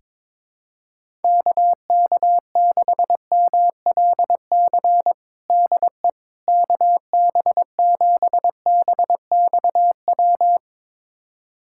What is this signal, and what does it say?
Morse code, heard as a keyed tone, about 21 wpm, 700 Hz. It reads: KK6MLC DE KB7BXW
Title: USA Callsign DE Callsign 22wpm